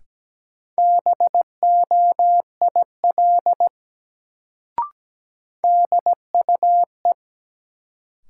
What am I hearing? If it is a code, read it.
Morse code, heard as a keyed tone, about 17 wpm, 700 Hz. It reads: BOIL DUE
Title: Sets of 5 Words Encore - New Word Drill - 17wpm